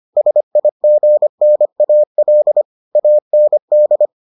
Morse code, heard as a keyed tone, about 25 wpm, 600 Hz. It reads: SIGNAL AND